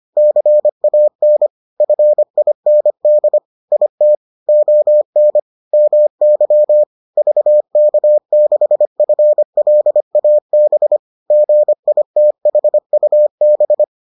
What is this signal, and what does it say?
Morse code, heard as a keyed tone, about 25 wpm, 600 Hz. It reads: CAN FIND IT ON MY VK6FLAB GITHUB